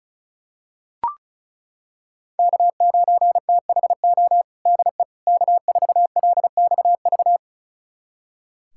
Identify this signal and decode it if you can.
Morse code, heard as a keyed tone, about 34 wpm, 700 Hz. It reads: K9THO DE K4LXV
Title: USA Callsign DE Callsign 35wpm